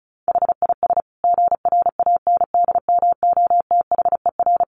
Morse code, heard as a keyed tone, about 34 wpm, 700 Hz. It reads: HIS GRANDMOTHER
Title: Sentences from Top 1200 Words 35wpm